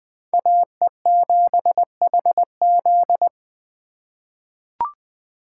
Morse code, heard as a keyed tone, about 20 wpm, 700 Hz. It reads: AE7HZ
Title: USA Call Signs - 5 Characters - 3x Character Spacing - 20wpm